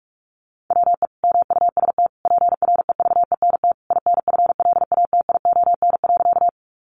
Morse code, heard as a keyed tone, about 46 wpm, 700 Hz. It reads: WE MUST PREVENT INFLATION.
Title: Sentences from Top 1100 Words 45wpm